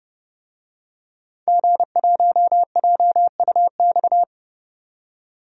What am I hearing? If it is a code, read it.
Morse code, heard as a keyed tone, about 30 wpm, 700 Hz. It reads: G1JUX
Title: DXCC-100 Call Signs - 5 Characters - 30wpm